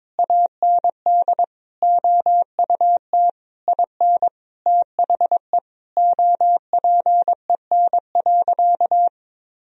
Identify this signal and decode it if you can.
Morse code, heard as a keyed tone, about 22 wpm, 700 Hz. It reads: AND OUT IN THE OPEN.
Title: Sentences from Top 1000 Words - Review Part 1 - 22wpm